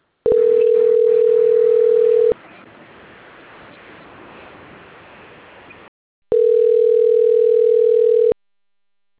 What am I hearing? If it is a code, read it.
Ringback tone (North America), 440 + 480 Hz.